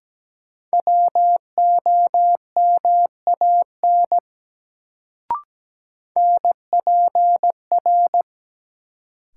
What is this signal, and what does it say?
Morse code, heard as a keyed tone, about 17 wpm, 700 Hz. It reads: WOMAN NPR
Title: Sets of 5 Words - New Word Drill - 17wpm